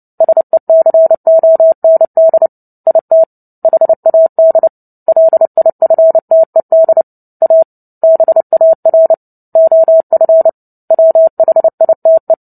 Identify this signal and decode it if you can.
Morse code, heard as a keyed tone, about 30 wpm, 650 Hz. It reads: SECOND IT HAD LIFTED A BAR OF WHITE
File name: Book Two / 03-Chapter 3 - 29 WPM